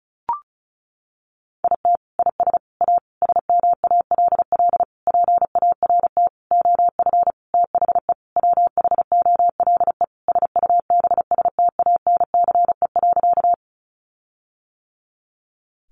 Morse code, heard as a keyed tone, about 34 wpm, 700 Hz. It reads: IT IS A SMALL PART OF THE WHOLE SUBSTANCE.
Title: Sentences from Top 1500 Words 35wpm